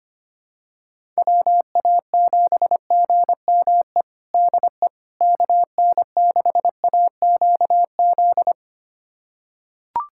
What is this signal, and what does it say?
Morse code, heard as a keyed tone, about 25 wpm, 700 Hz. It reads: WA7GME DE KN6AQZ
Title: USA Callsign DE Callsign 25wpm